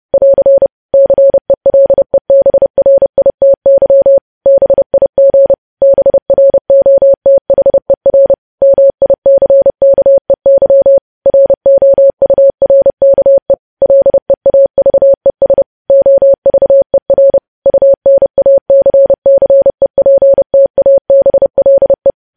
Morse code, heard as a keyed tone, about 30 wpm, 550 Hz. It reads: <AR> CELEBRITY BIG BROTHER MICKEY ROURKE LEAVES OVER UNACCEPTABLE
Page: news headlines in morse code at 30 wpm